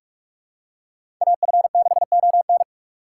Morse code, heard as a keyed tone, about 48 wpm, 700 Hz. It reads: AF6ON
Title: USA Call Signs - 5 Characters - 2x Character Spacing - 45wpm